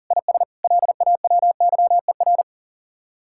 Morse code, heard as a keyed tone, about 48 wpm, 700 Hz. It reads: IS LAWYER
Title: Speed-Racing Common QSO Phrases - 40wpm